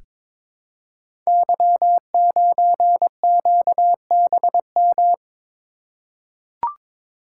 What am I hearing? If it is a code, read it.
Morse code, heard as a keyed tone, about 22 wpm, 700 Hz. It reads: Y9QBM
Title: Code Groups - 5 Letter:Numbers - 22wpm